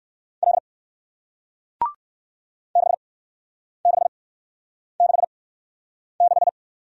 Morse code, heard as a keyed tone, about 57 wpm, spaced as for 9 wpm, 700 Hz. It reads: L 6666